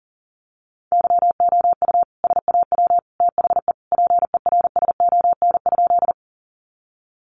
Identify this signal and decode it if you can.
Morse code, heard as a keyed tone, about 40 wpm, 700 Hz. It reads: YOU SAW THE PERSON?
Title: Sentences from Top 400 Words 40wpm